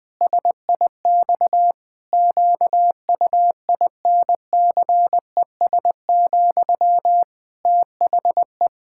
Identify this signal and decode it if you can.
Morse code, heard as a keyed tone, about 21 wpm, 700 Hz. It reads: SIX QUINCES, THE
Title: Pangram Challenge v2 - 20wpm